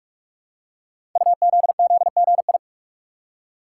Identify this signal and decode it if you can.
Morse code, heard as a keyed tone, about 44 wpm, 700 Hz. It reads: UZZGI
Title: Code Groups - 5 Letters - 45wpm